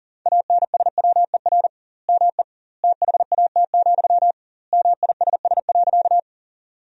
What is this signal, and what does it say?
Morse code, heard as a keyed tone, about 40 wpm, 700 Hz. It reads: ANSWER ME THAT, MISS.
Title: 2x Word-Spacing - Sentences from Top 500 Words - 40wpm